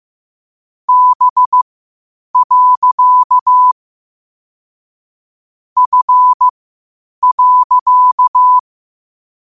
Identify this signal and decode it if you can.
Morse code, heard as a keyed tone, about 15 wpm, spaced as for 8 wpm, 1000 Hz. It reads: B. F.
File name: qotdfiles / 8wpm